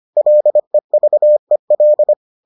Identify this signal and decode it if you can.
Morse code, heard as a keyed tone, about 25 wpm, 600 Hz. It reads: LEVEL